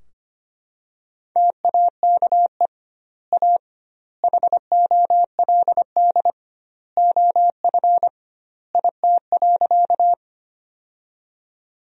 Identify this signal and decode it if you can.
Morse code, heard as a keyed tone, about 25 wpm, 700 Hz. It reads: TAKE A HOLD OF IT.
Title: 2x Word-Spacing - Sentences from Top 400 Words - 25wpm